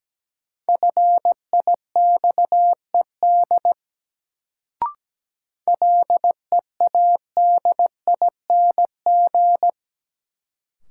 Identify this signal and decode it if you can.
Morse code, heard as a keyed tone, about 17 wpm, 700 Hz. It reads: FIXED LEADING
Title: Sentences from Top 1100 Words - New Word Drill - 17wpm